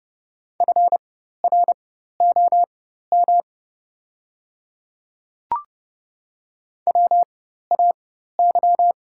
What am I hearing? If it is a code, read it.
Morse code, heard as a keyed tone, about 30 wpm, spaced as for 14 wpm, 700 Hz. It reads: FROM WAY
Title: 4x Character Spacing - Top 100 Words - 30wpm